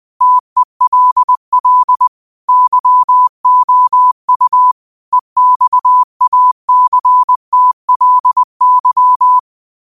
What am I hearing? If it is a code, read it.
Morse code, heard as a keyed tone, about 20 wpm, 1000 Hz. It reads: TELL YOU EXACTLY
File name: 20wpm